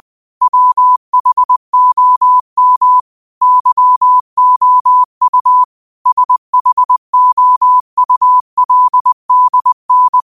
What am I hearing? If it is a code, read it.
Morse code, heard as a keyed tone, about 20 wpm, 1000 Hz. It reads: WHOM YOU SHOULDN
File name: qotdfiles / 20wpm